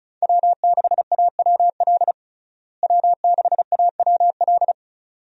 Morse code, heard as a keyed tone, about 34 wpm, 700 Hz. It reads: W6AWL W6AWL
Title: USA Call Signs - 5 Characters - Repeated 2x - 35wpm